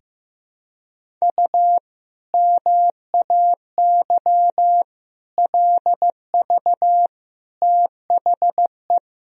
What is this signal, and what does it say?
Morse code, heard as a keyed tone, about 16 wpm, 700 Hz. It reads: U MAY LV THE
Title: Sentences from Top 400 Words and Abbreviations 15wpm